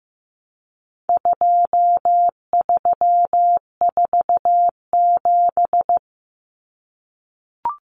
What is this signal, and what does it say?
Morse code, heard as a keyed tone, about 15 wpm, 700 Hz. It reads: 2347
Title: Numbers - 4-Digit 15wpm